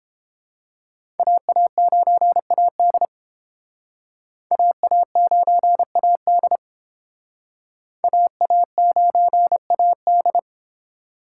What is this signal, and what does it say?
Morse code, heard as a keyed tone, about 30 wpm, 700 Hz. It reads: AA9AD AA9AD AA9AD